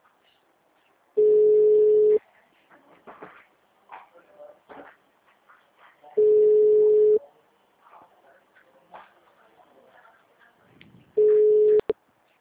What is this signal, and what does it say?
Ringback tone (Europe, 425 Hz), with speech.
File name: out- -2013- - - .268